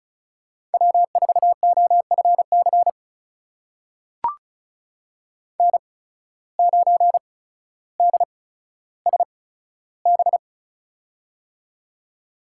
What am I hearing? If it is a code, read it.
Morse code, heard as a keyed tone, about 34 wpm, spaced as for 15 wpm, 700 Hz. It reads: W4OFC N9DSB
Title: USA Call Signs - 5 Characters - 8x Character Spacing - 35wpm